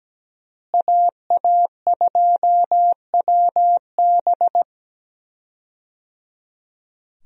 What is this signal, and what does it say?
Morse code, heard as a keyed tone, about 17 wpm, 700 Hz. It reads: AA2WB
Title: USA Call Signs - 5 Characters - Repeated 3x - 17wpm